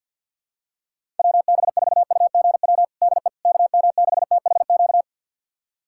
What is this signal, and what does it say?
Morse code, heard as a keyed tone, about 46 wpm, 700 Hz. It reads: WB4UGW DE KM6THQ
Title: USA Callsign DE Callsign - 6 Characters - 50wpm